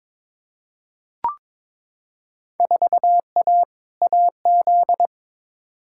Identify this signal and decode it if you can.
Morse code, heard as a keyed tone, about 22 wpm, 700 Hz. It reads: 4A AZ
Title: Field Day Exchanges - 22wpm